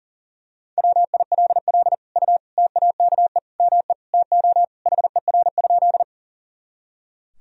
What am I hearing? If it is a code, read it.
Morse code, heard as a keyed tone, about 40 wpm, 700 Hz. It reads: WILL U TAKE ME TO HER?
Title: Sentences from Top 200 Words and Abbreviations 40wpm